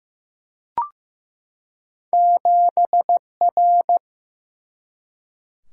Morse code, heard as a keyed tone, about 15 wpm, 700 Hz. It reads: 7 R
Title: Single-Letter Single-Number 15wpm